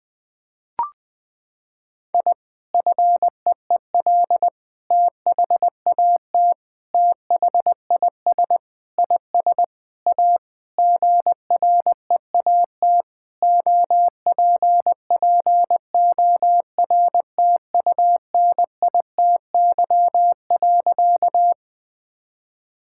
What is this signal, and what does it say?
Morse code, heard as a keyed tone, about 20 wpm, 700 Hz. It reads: I FEEL THAT THIS IS A GREAT OPPORTUNITY.
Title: Sentences with 4-Word Phrases - 20wpm